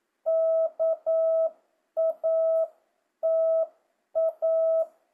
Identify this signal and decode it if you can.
Morse code, heard as a keyed tone, about 9 wpm, 650 Hz. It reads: KATA
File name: jh8cbh / 2025cw